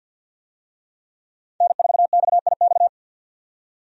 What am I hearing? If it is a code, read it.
Morse code, heard as a keyed tone, about 51 wpm, 700 Hz. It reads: N4XIX
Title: USA Call Signs - 5 Characters - 50wpm